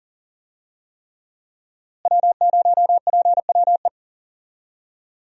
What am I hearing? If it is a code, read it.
Morse code, heard as a keyed tone, about 40 wpm, 700 Hz. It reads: W0PWE
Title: USA Call Signs - 5 Characters - Repeated 3x - 40wpm